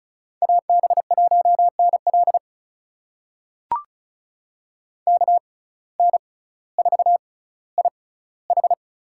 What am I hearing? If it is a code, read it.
Morse code, heard as a keyed tone, about 36 wpm, spaced as for 18 wpm, 700 Hz. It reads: AB1NL KN4IH
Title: USA Call Signs - 6x Character Spacing - 35wpm